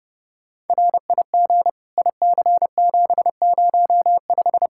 Morse code, heard as a keyed tone, about 29 wpm, 700 Hz. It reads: RIG IC705
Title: K1USN SST QSO Practice 30wpm